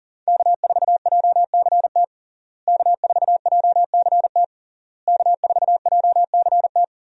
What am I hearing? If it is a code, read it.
Morse code, heard as a keyed tone, about 40 wpm, 700 Hz. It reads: K4JCT K4JCT K4JCT
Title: USA Call Signs - 5 Characters - Repeated 3x - 40wpm